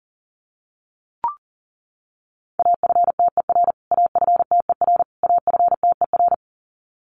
Morse code, heard as a keyed tone, about 40 wpm, 700 Hz. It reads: AFTER AFTER AFTER
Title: Top 500 Words Repeated 40wpm